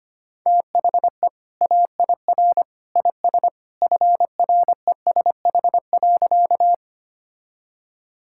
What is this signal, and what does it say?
Morse code, heard as a keyed tone, about 25 wpm, 700 Hz. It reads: THE AIR IS FRESH.
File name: practice / 25wpm / Sentences with 2-Word Phrases - 25wpm